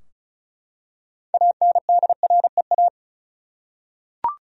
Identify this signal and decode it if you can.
Morse code, heard as a keyed tone, about 34 wpm, 700 Hz. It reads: ANDREA
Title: USA Common Names - 35wpm